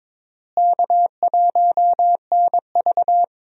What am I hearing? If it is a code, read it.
Morse code, heard as a keyed tone, about 22 wpm, 700 Hz. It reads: K1NV
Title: USA Call Signs - 4 Characters - Repeated 3x - 22wpm